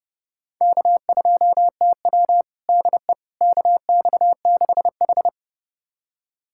Morse code, heard as a keyed tone, about 31 wpm, 700 Hz. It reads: K2TW DE KX6H
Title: USA Callsign DE Callsign - 4 Characters - 30wpm